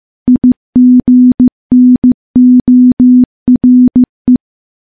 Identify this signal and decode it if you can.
Morse code, heard as a keyed tone, about 15 wpm, 250 Hz. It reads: IGNORE
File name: morse-training-15WPM